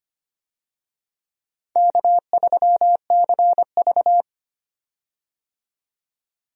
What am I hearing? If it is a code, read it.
Morse code, heard as a keyed tone, about 25 wpm, 700 Hz. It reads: K3CV
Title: USA Call Signs - 4 Characters - 25wpm